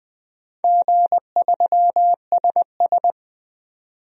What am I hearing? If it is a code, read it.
Morse code, heard as a keyed tone, about 20 wpm, 700 Hz. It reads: G3SS
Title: DXCC-100 Call Signs - 4 Characters - 20wpm